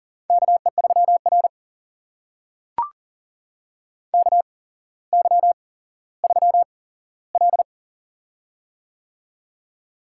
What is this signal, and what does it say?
Morse code, heard as a keyed tone, about 44 wpm, spaced as for 17 wpm, 700 Hz. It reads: KE3R KY3L
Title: USA Call Signs - 4 Characters - 8x Character Spacing - 40wpm